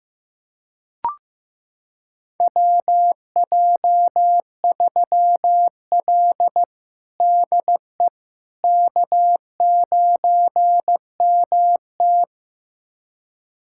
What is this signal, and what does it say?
Morse code, heard as a keyed tone, about 15 wpm, 700 Hz. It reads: WJ3L DE K9MT
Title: USA Callsign DE Callsign - 4 Characters - 15wpm